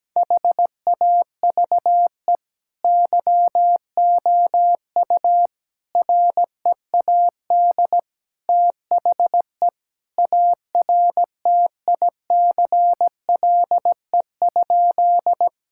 Morse code, heard as a keyed tone, about 17 wpm, 700 Hz. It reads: HAVE YOU READ THE ARTICLE?
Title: Sentences from Top 1000 Words - Review Part 5 - 17wpm